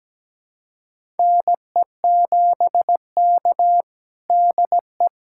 Morse code, heard as a keyed tone, about 17 wpm, 700 Hz. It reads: NE7K DE
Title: USA Callsign DE Callsign - 4 Characters - 17wpm